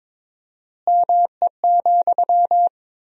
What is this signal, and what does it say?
Morse code, heard as a keyed tone, about 22 wpm, 700 Hz. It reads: ME,
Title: 4x Word-Spacing - Sentences from Top 200 Words - 22wpm